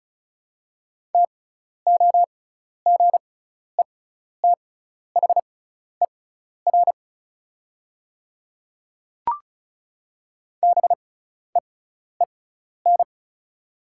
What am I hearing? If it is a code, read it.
Morse code, heard as a keyed tone, about 36 wpm, spaced as for 12 wpm, 700 Hz. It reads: TOGETHER BEEN
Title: 6x Character Spacing - Top 500 Words - 35wpm